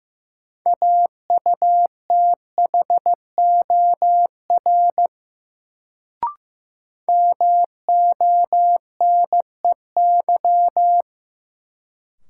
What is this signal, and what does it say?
Morse code, heard as a keyed tone, about 15 wpm, 700 Hz. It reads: AUTHOR MONEY
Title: Sets of 4 Words - New Word Drill - 15wpm